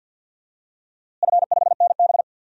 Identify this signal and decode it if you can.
Morse code, heard as a keyed tone, about 50 wpm, 700 Hz. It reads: F5NB
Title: DXCC-100 Call Signs - 4 Characters - Repeated 2x - 50wpm